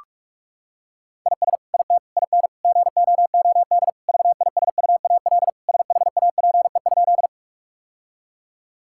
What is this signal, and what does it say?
Morse code, heard as a keyed tone, about 41 wpm, 700 Hz. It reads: IS IT IN GOOD VISUAL SHAPE?
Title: Sentences from Top 1000 Words - Review Part 6 - 45wpm